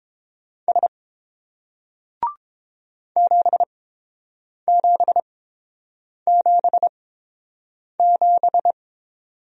Morse code, heard as a keyed tone, about 31 wpm, spaced as for 7 wpm, 700 Hz. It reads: S 7777